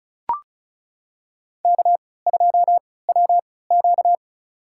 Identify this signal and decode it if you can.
Morse code, heard as a keyed tone, about 36 wpm, spaced as for 20 wpm, 700 Hz. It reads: K2WQ
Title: USA Call Signs - 4 Characters - 3x Character Spacing - 35wpm